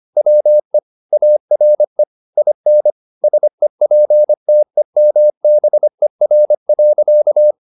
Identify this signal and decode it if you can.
Morse code, heard as a keyed tone, about 25 wpm, 600 Hz. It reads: WE ARE IN SEPTEMBER.
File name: the old man and the sea part 1 25wpm